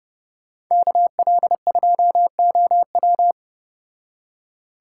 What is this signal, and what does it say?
Morse code, heard as a keyed tone, about 35 wpm, 700 Hz. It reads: KL2OW